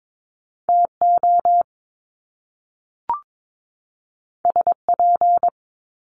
Morse code, heard as a keyed tone, about 22 wpm, 700 Hz. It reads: TO SP
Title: Top Two Letter Combinations in Words 22wpm